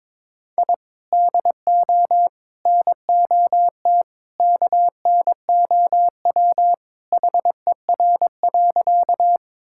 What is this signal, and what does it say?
Morse code, heard as a keyed tone, about 22 wpm, 700 Hz. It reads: I DO NOT KNOW HER.
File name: practice / 22wpm / Sentences with 4-Word Phrases - 22wpm